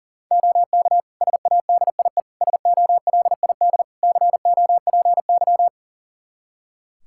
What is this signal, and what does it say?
Morse code, heard as a keyed tone, about 37 wpm, 690 Hz. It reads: OK SADIE SOLID COPY
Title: Practice QSO - Exchange 3 - More Info - 40wpm